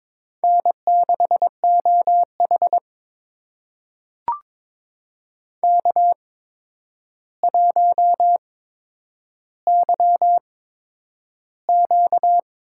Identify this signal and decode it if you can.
Morse code, heard as a keyed tone, about 22 wpm, spaced as for 9 wpm, 700 Hz. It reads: N6OH K1YQ
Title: USA Call Signs - 4 Characters - 8x Character Spacing - 22wpm